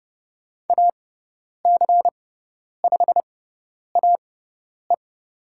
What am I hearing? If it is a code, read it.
Morse code, heard as a keyed tone, about 29 wpm, spaced as for 10 wpm, 700 Hz. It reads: AC5AE
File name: Lesson 16g - Letter c - Call Signs 30wpm